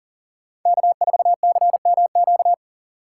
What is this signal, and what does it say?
Morse code, heard as a keyed tone, about 40 wpm, 700 Hz. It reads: K4CMQ
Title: USA Call Signs - 3x Character Spacing - 40wpm